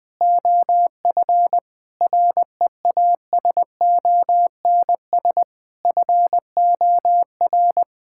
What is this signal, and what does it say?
Morse code, heard as a keyed tone, about 20 wpm, 700 Hz. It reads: OF REASONS FOR
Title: Sets of 8 Words - 20wpm